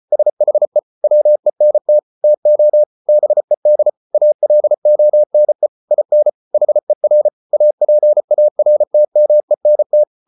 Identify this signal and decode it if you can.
Morse code, heard as a keyed tone, about 34 wpm, 600 Hz. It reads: SHE WENT TO BED ALONE IN HER APARTMENT